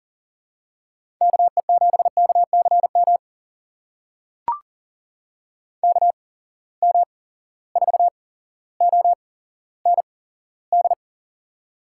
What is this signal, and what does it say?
Morse code, heard as a keyed tone, about 40 wpm, spaced as for 17 wpm, 700 Hz. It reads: KE7KCM KM4OND
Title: USA Call Signs - 6 Characters - 8x Character Spacing - 40wpm